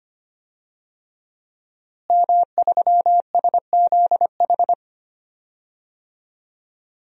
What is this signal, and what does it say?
Morse code, heard as a keyed tone, about 25 wpm, 700 Hz. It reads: M3SZH
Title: DXCC-100 Call Signs - 5 Characters - Repeated 3x - 25wpm